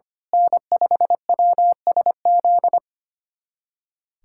Morse code, heard as a keyed tone, about 25 wpm, 700 Hz. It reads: N5WSZ
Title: USA Callsign DE Callsign - 5 Characters - 25wpm